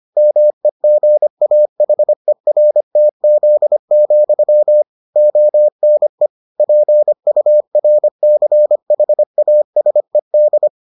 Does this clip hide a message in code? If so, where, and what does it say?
Yes: Morse code from 0.2 to 10.7 s, heard as a keyed tone, about 25 wpm, 600 Hz. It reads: MEGAHERTZ, ONE PURCHASED